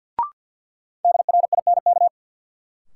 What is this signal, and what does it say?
Morse code, heard as a keyed tone, about 48 wpm, 700 Hz. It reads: DRINK
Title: Sentences from Top 1500 Words - New Word Drill Alt - 50wpm